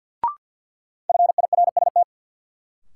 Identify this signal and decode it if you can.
Morse code, heard as a keyed tone, about 54 wpm, 700 Hz. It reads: FIRST
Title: Sentences from Top 100 Words - New Word Drill - 50wpm